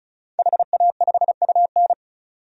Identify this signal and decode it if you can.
Morse code, heard as a keyed tone, about 36 wpm, 700 Hz. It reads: HA5UN
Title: DXCC-100 Call Signs - 5 Characters - 35wpm